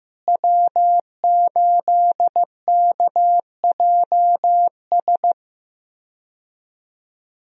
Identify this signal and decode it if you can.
Morse code, heard as a keyed tone, about 15 wpm, 700 Hz. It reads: W8KJS
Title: USA Call Signs - 5 Characters - Repeated 3x - 15wpm